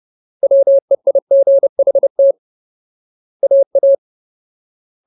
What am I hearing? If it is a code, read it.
Morse code, heard as a keyed tone, about 30 wpm, 550 Hz. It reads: WEIGHT AA
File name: Ultimate Word List / 30WPM Ultimate